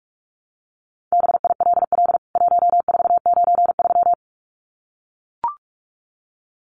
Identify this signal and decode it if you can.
Morse code, heard as a keyed tone, about 44 wpm, 700 Hz. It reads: BILL 1493
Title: CWT Exchange 45wpm